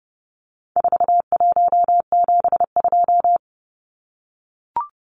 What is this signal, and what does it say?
Morse code, heard as a keyed tone, about 30 wpm, 700 Hz. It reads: 4172
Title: Numbers - 4-Digit 30wpm